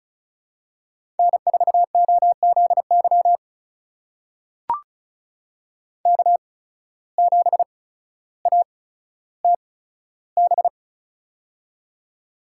Morse code, heard as a keyed tone, about 36 wpm, spaced as for 15 wpm, 700 Hz. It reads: N4OZY K7ATB
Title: USA Call Signs - 5 Characters - 8x Character Spacing - 35wpm